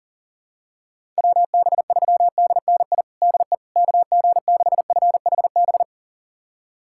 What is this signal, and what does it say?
Morse code, heard as a keyed tone, about 40 wpm, 700 Hz. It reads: WB3DNI DE KG6FHB
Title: USA Callsign DE Callsign - 6 Characters - 40wpm